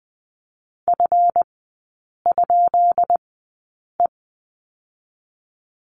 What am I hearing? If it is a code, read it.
Morse code, heard as a keyed tone, about 20 wpm, 700 Hz. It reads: F ? E
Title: 218 Lesson 35e - Number 0 - Three Characters 20wpm